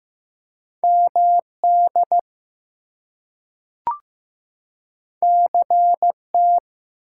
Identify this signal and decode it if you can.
Morse code, heard as a keyed tone, about 15 wpm, 700 Hz. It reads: MD CT
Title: US States and Canadian Provinces 15wpm